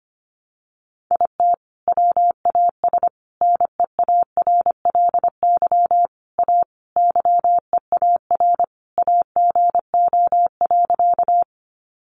Morse code, heard as a keyed tone, about 25 wpm, 700 Hz. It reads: IT WAS NEARLY A YEAR AGO.
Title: Sentences from Top 1300 Words 25wpm